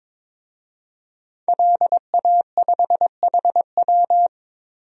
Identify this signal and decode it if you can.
Morse code, heard as a keyed tone, about 22 wpm, 700 Hz. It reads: LA5HW
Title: DXCC-100 Call Signs - 5 Characters - Repeated 3x - 22wpm